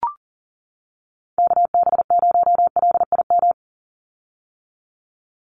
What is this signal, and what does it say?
Morse code, heard as a keyed tone, about 40 wpm, 700 Hz. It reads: KB0LIM
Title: USA Call Signs 40wpm